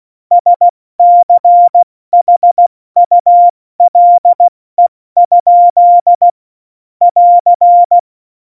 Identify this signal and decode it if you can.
Morse code, heard as a keyed tone, about 16 wpm, spaced as for 14 wpm, 700 Hz. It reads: SCHULE? <AR>
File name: Morsekurs / Witze / Witz 30. Januar 2018, 70 F80